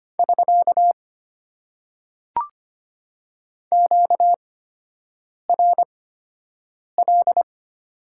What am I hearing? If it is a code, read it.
Morse code, heard as a keyed tone, about 25 wpm, spaced as for 7 wpm, 700 Hz. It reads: <SK> QRL
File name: Top 90 QSO Elements - 8x Character Spacing - 25wpm